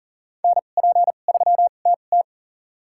Morse code, heard as a keyed tone, about 40 wpm, spaced as for 29 wpm, 700 Hz. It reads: NP3TT
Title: USA Call Signs - 5 Characters - 2x Character Spacing - 40wpm